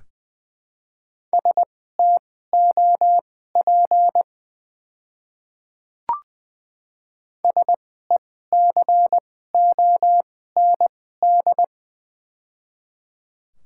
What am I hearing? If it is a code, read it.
Morse code, heard as a keyed tone, about 21 wpm, spaced as for 15 wpm, 700 Hz. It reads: STOP SECOND
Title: 2x Character Spacing - Top 300 Words - 20wpm